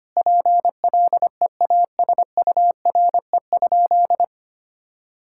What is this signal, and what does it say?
Morse code, heard as a keyed tone, about 25 wpm, 700 Hz. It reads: PLEASURE?
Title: Sentences from Top 1600 Words 25wpm